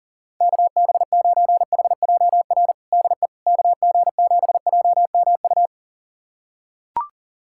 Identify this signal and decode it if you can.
Morse code, heard as a keyed tone, about 40 wpm, 700 Hz. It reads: KB9HJR DE KG7JMU
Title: USA Callsign DE Callsign - 6 Characters - 40wpm